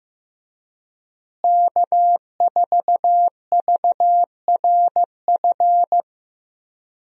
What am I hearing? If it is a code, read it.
Morse code, heard as a keyed tone, about 15 wpm, 700 Hz. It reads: K4VRF
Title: USA Call Signs - 5 Characters - Repeated 2x - 15wpm